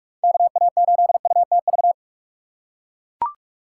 Morse code, heard as a keyed tone, about 44 wpm, 700 Hz. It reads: KA8UTV
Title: USA Call Signs - 6 Characters - 2x Character Spacing - 45wpm